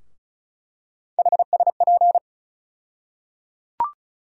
Morse code, heard as a keyed tone, about 36 wpm, 700 Hz. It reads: HSP
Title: USA Call Sign Postfixes - 3 Letters 35wpm